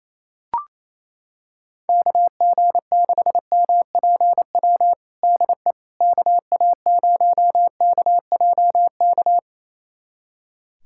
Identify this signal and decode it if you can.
Morse code, heard as a keyed tone, about 27 wpm, 700 Hz. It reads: KG6MPW DE KA0KJK
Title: USA Callsign DE Callsign - 6 Characters - 28wpm